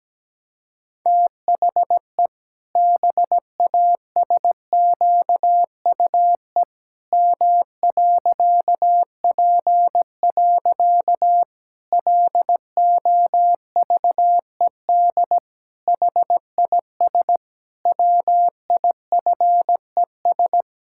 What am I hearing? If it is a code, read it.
Morse code, heard as a keyed tone, about 17 wpm, 700 Hz. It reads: THE BASQUE M.P. LOVED HIS WIFES
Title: Pangram Challenge v2 - 17wpm